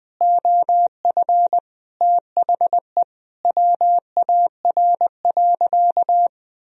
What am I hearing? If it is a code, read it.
Morse code, heard as a keyed tone, about 20 wpm, 700 Hz. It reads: OF THE WAR.
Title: Sentences from Top 1000 Words - Review Part 4 - 20wpm